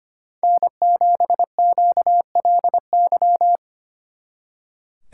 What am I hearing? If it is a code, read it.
Morse code, heard as a keyed tone, about 29 wpm, 700 Hz. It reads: N7QLY
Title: Speed Racing - USA 1x3 Call Signs 25wpm